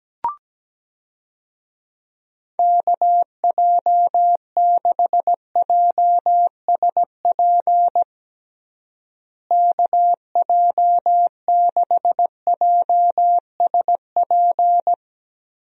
Morse code, heard as a keyed tone, about 17 wpm, 700 Hz. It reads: KJ6JSP KJ6JSP
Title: USA Call Signs - 6 Characters - Repeated 2x - 17wpm